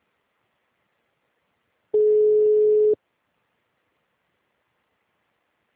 Ringback tone (Europe, 425 Hz).
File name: out- -2005- - - .367